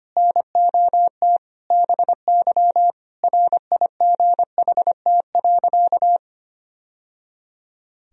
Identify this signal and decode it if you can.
Morse code, heard as a keyed tone, about 25 wpm, 700 Hz. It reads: NOT BY RIGHT.
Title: Sentences from Top 1000 Words - Review Part 3 - 25wpm